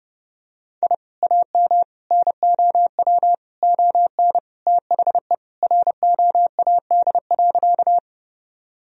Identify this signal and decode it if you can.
Morse code, heard as a keyed tone, about 30 wpm, 700 Hz. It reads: I AM NOW ON THE ROAD.
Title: Sentences from Top 1000 Words - Review Part 2 - 30wpm